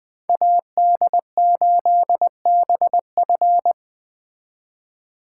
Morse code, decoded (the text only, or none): AD8BF